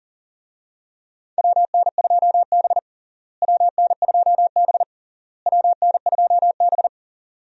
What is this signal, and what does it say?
Morse code, heard as a keyed tone, about 40 wpm, 700 Hz. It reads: WN2B WN2B WN2B
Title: USA Call Signs - 4 Characters - Repeated 3x - 40wpm